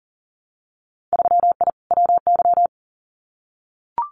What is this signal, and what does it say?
Morse code, heard as a keyed tone, about 40 wpm, 700 Hz. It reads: 3I WY
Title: Winter Field Day Exchanges Encore 40wpm